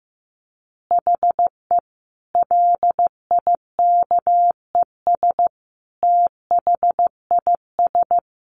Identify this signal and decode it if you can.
Morse code, heard as a keyed tone, about 15 wpm, 700 Hz. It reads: HE LIKES THIS
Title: Additional Easy Phrases 15wpm